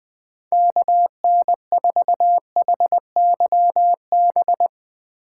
Morse code, heard as a keyed tone, about 21 wpm, 700 Hz. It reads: KN4HYB